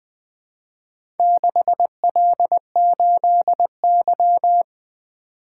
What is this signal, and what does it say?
Morse code, heard as a keyed tone, about 20 wpm, 700 Hz. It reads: 6L8Y
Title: Four Letter-Number Combinations - 20wpm